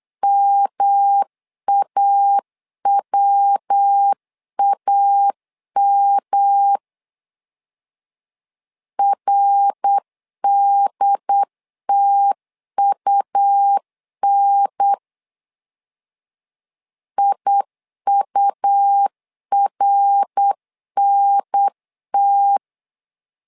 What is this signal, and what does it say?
Morse code, heard as a keyed tone, about 8 wpm, 800 Hz. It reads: MAWAM RDTUN IURNT
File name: LEZIONI / 3 MWRO / 3 MWR 8wpm 2444